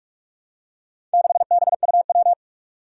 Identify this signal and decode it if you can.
Morse code, heard as a keyed tone, about 44 wpm, 700 Hz. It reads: 6BUW
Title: Four Challenging Letter-Number Combinations 45wpm